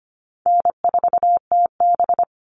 Morse code, heard as a keyed tone, about 25 wpm, 700 Hz. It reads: N4TB
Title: Easy Call Signs - 25wpm